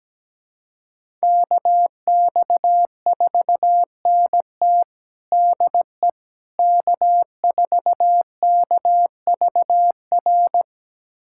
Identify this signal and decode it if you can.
Morse code, heard as a keyed tone, about 17 wpm, 700 Hz. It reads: KX4NT DE K4KVR
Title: USA Callsign DE Callsign - 5 Characters - 17wpm